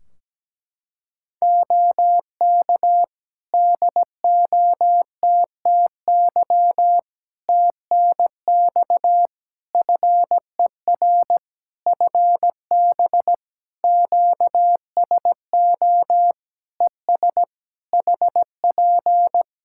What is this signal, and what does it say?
Morse code, heard as a keyed tone, about 17 wpm, 700 Hz. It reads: OK DOTTY TNX FER FB QSO ES HP
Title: Practice QSO - Exchange 4 - Ending - 17wpm